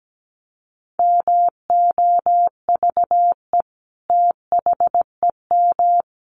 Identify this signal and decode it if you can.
Morse code, heard as a keyed tone, about 17 wpm, 700 Hz. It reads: MOVE THEM
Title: Sentences from Top 1400 Words 17wpm